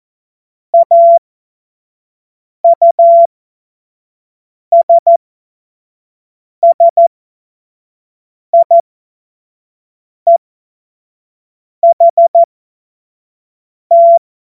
Morse code, heard as a keyed tone, about 14 wpm, spaced as for 5 wpm, 680 Hz. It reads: AUSSIEHT
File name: Morsekurs / Witze / Witz 24.Oktober 2023 25 F70